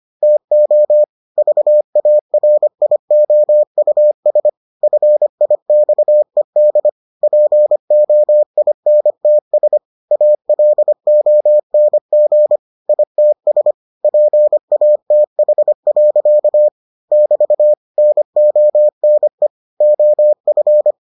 Morse code, heard as a keyed tone, about 25 wpm, 600 Hz. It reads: TO VARIOUS FIXED POINTS ALONG ITS PATH. <BT> NONE OF